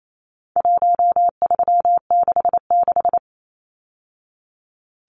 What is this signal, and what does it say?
Morse code, heard as a keyed tone, about 28 wpm, 700 Hz. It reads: 1366
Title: Numbers - Contest Style - 4-Digit 28wpm